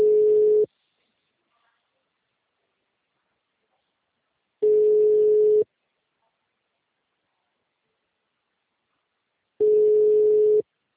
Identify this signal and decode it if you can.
Ringback tone (Europe, 425 Hz).
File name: out- -2002- - - .28